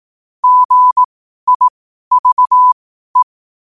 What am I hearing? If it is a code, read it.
Morse code, heard as a keyed tone, about 18 wpm, spaced as for 12 wpm, 1000 Hz. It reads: GIVE
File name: qotdfiles / 12wpm